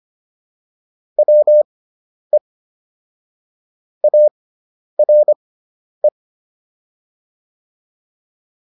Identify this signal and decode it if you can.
Morse code, heard as a keyed tone, about 25 wpm, spaced as for 10 wpm, 600 Hz. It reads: WE ARE